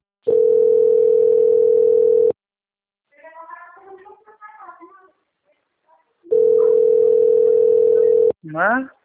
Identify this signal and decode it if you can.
Ringback tone (North America), 440 + 480 Hz, with speech.
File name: out- - - - - .12152